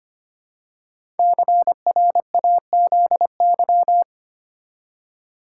Morse code, heard as a keyed tone, about 25 wpm, 700 Hz. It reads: CRAZY
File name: 5 Letter Words - First 3 Letters Given - 25wpm